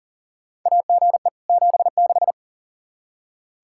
Morse code, heard as a keyed tone, about 39 wpm, 700 Hz. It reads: AGE 76
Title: K1USN SST QSO Practice 40wpm